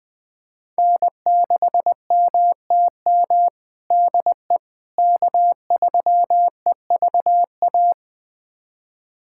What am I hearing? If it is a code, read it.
Morse code, heard as a keyed tone, about 20 wpm, 700 Hz. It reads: N6MTM DE K3EVA
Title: USA Callsign DE Callsign - 5 Characters - 20wpm